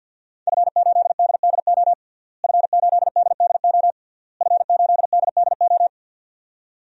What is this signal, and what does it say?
Morse code, heard as a keyed tone, about 47 wpm, 700 Hz. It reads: F8DDO F8DDO F8DDO
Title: DXCC-100 Call Signs - 5 Characters - Repeated 3x - 50wpm